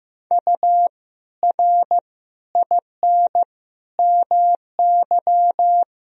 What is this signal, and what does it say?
Morse code, heard as a keyed tone, about 15 wpm, 700 Hz. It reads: U R IN MY
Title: Sentences from Top 500 Words and Abbreviations 15wpm